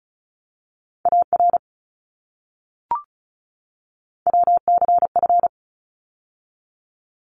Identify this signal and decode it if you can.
Morse code, heard as a keyed tone, about 34 wpm, 700 Hz. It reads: AR WCF
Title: ARRL Sections 35wpm